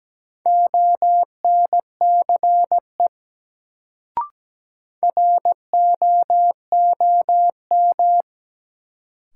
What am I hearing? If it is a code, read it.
Morse code, heard as a keyed tone, about 17 wpm, 700 Hz. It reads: ONCE ROOM
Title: Sentences from Top 300 Words - New Word Drill - 17wpm